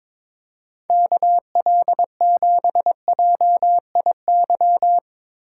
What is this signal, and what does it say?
Morse code, heard as a keyed tone, about 23 wpm, 700 Hz. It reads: KL7JIY
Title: USA Call Signs - 6 Characters - 3x Character Spacing - 22wpm